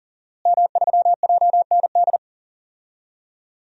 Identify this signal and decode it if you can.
Morse code, heard as a keyed tone, about 40 wpm, 700 Hz. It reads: M3JND
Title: DXCC-100 Call Signs - 5 Characters - Repeated 3x - 40wpm